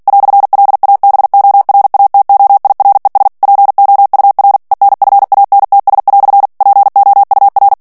Morse code, heard as a keyed tone, about 48 wpm, 800 Hz. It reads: "LABORATOIRES POUR ENFANTS" POUR
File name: CW / FICHIERS-MP3 / FRANCAIS TEXTE CLAIR 1 VITESSE 45 MOTS MINUTES